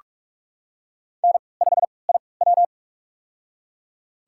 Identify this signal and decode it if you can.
Morse code, heard as a keyed tone, about 42 wpm, spaced as for 25 wpm, 700 Hz. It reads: N5IW
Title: USA Call Signs - 4 Characters - 3x Character Spacing - 45wpm